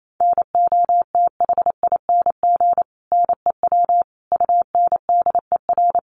Morse code, heard as a keyed tone, about 28 wpm, 700 Hz. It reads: NOTHING NEW UNDER
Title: Sentences from Top 500 Words 28wpm